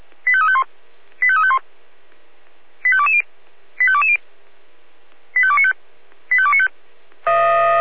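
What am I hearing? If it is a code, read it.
Five-tone selective call (ZVEI), heard as quick runs of five tones, tones of about 70 ms. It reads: 85351 85351 85220 85220 85295 85295
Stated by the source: Five-tone ZVEI selective call received by SDR.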